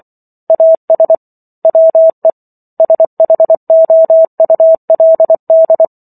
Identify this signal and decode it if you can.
Morse code, heard as a keyed tone, about 24 wpm, 650 Hz. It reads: AS WE SHOULD